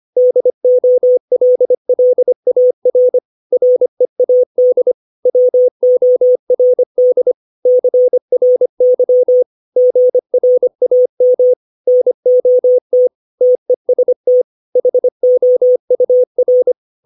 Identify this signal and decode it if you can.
Morse code, heard as a keyed tone, about 25 wpm, 500 Hz. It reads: DOLLAR READ WORD CRY GRAM NOT TEST HOUR
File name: 632 words / 632 Words 25WPM